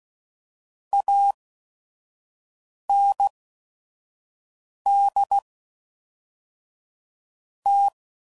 Morse code, heard as a keyed tone, about 16 wpm, spaced as for 5 wpm, 800 Hz. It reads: AND T
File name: CW Code Groups @5WPM / QSOs / kc7sac